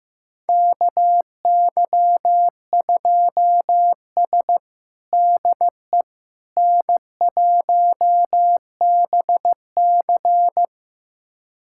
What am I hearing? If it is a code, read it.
Morse code, heard as a keyed tone, about 16 wpm, 700 Hz. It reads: KY2S DE N1BC
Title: USA Callsign DE Callsign - 4 Characters - 15wpm